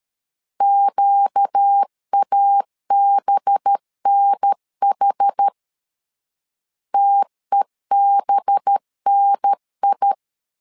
Morse code, heard as a keyed tone, about 13 wpm, 800 Hz. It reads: QABNH TEBNI
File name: LEZIONI / 5 LBVC / 5 LB 12wpm 1906